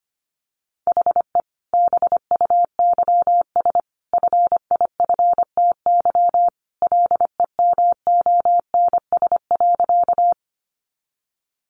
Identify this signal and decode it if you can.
Morse code, heard as a keyed tone, about 25 wpm, 700 Hz. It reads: HE BUYS FIFTY LEMONS.
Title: Additional Easy Phrases 25wpm